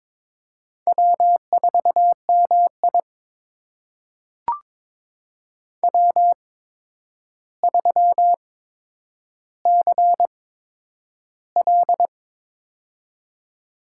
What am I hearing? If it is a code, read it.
Morse code, heard as a keyed tone, about 22 wpm, spaced as for 9 wpm, 700 Hz. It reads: W4MI W3CL
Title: USA Call Signs - 4 Characters - 8x Character Spacing - 22wpm